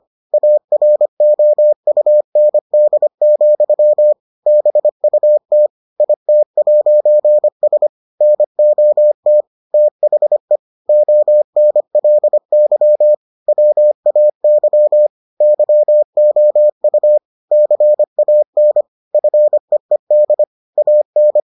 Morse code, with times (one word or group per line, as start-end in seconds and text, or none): 0.3-4.1 AROUND,
4.5-5.7 BUT
6.0-7.9 IT'S
8.2-9.4 NOT
9.7-10.6 THE
10.9-13.1 ONLY
13.5-15.1 WAY
15.4-17.2 YOU
17.5-18.8 CAN
19.1-20.4 FEED
20.8-21.4 AN